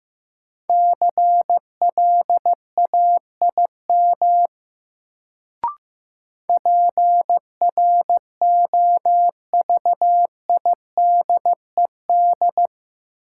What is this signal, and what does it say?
Morse code, heard as a keyed tone, about 15 wpm, 700 Hz. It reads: CLAIM PROVIDED
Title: Sentences from Top 1500 Words - New Word Drill Alt - 15wpm